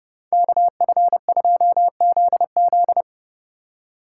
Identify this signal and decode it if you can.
Morse code, heard as a keyed tone, about 30 wpm, 700 Hz. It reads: KF2ZZ